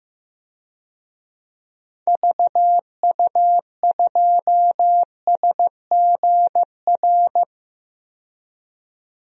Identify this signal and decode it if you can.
Morse code, heard as a keyed tone, about 16 wpm, 700 Hz. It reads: VU2SGR
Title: DXCC-100 Call Signs - 6 Characters - 15wpm